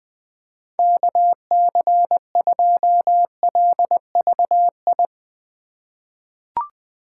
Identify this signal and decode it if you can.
Morse code, heard as a keyed tone, about 20 wpm, 700 Hz. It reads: KC2LVI
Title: USA Call Signs - 6 Characters - 8x Character Spacing - 20wpm